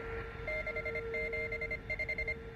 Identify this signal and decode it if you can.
Morse code, heard as a keyed tone, about 25 wpm, 2100 Hz, with noise. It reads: 675